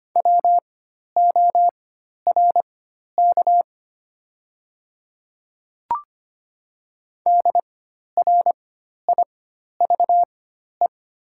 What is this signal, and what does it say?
Morse code, heard as a keyed tone, about 25 wpm, spaced as for 12 wpm, 700 Hz. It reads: WORK DRIVE
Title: 4x Character Spacing - Top 500 Words - 25wpm